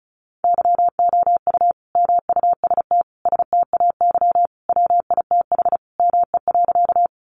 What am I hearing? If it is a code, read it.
Morse code, heard as a keyed tone, about 34 wpm, 700 Hz. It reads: YOU MUST STAY WITH ME.
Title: Sentences from Top 500 Words 35wpm